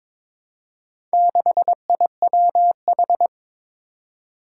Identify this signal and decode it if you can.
Morse code, heard as a keyed tone, about 22 wpm, 700 Hz. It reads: 6IWH
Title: Four Challenging Letter-Number Combinations 22wpm